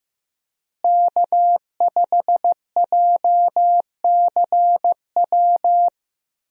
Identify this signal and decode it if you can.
Morse code, heard as a keyed tone, about 14 wpm, 700 Hz. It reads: K5JCW
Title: USA Call Signs - 5 Characters - Repeated 3x - 15wpm